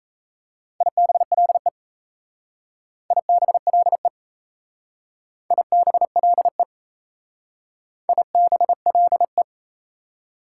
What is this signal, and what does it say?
Morse code, heard as a keyed tone, about 33 wpm, 700 Hz. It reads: IBLE IBLE IBLE IBLE
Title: Speed-Racing Prefixes and Suffixes 28wpm